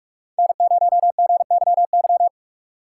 Morse code, heard as a keyed tone, about 42 wpm, 700 Hz. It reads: N0GYY